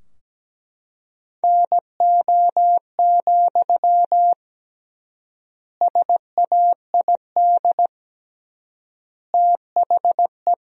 Morse code, heard as a keyed tone, about 17 wpm, 700 Hz. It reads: NO, SAID THE
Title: 3x Word-Spacing - Sentences from Top 300 Words - 17wpm